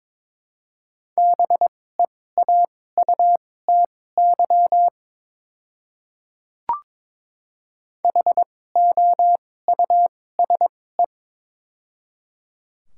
Morse code, heard as a keyed tone, about 22 wpm, spaced as for 16 wpm, 700 Hz. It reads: BEAUTY HOUSE
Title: 2x Character Spacing - Top 500 Words - 22wpm